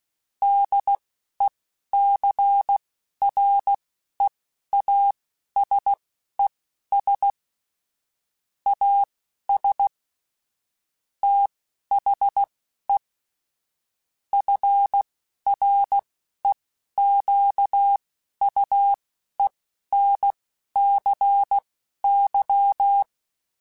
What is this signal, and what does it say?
Morse code, heard as a keyed tone, about 16 wpm, spaced as for 11 wpm, 800 Hz. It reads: DECREASES AS THE FREQUENCY
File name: Technical Subjects / Technical Subjects-mp3s / XC @10